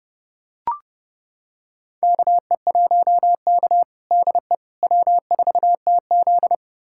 Morse code, heard as a keyed tone, about 30 wpm, 700 Hz. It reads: KE1K DE W4TZ
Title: USA Callsign DE Callsign - 4 Characters - 30wpm